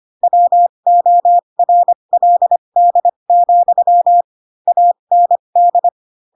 Morse code, heard as a keyed tone, about 25 wpm, spaced as for 22 wpm, 700 Hz. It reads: WORLD, AND
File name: Pinocchio / Pinocchio ch4 18 25 ik1hge